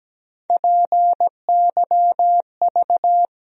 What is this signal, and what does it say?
Morse code, heard as a keyed tone, about 17 wpm, 700 Hz. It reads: PYV
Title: USA Call Sign Postfixes - 3 Letters 17wpm